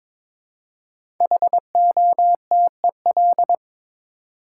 Morse code, heard as a keyed tone, about 22 wpm, 700 Hz. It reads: HOTEL
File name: 5 Letter Words - First 3 Letters Given - 22wpm